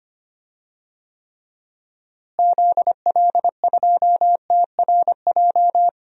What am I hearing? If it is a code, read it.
Morse code, heard as a keyed tone, about 25 wpm, 700 Hz. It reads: ZL2TRJ
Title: DXCC-100 Call Signs - 6 Characters - Repeated 2x - 25wpm